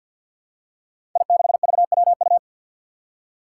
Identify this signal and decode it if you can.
Morse code, heard as a keyed tone, about 47 wpm, 700 Hz. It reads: I6VWU
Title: Five Challenging Letter-Number Combinations 50wpm